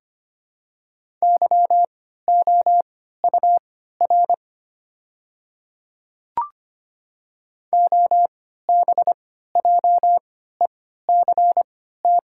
Morse code, heard as a keyed tone, about 25 wpm, spaced as for 14 wpm, 700 Hz. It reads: YOUR OBJECT
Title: 3x Character Spacing - Top 500 Words - 25wpm